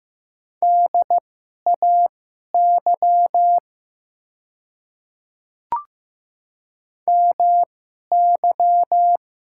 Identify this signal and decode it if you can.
Morse code, heard as a keyed tone, about 15 wpm, spaced as for 11 wpm, 700 Hz. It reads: DAY MY
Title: 2x Character Spacing - Top 100 Words - 15wpm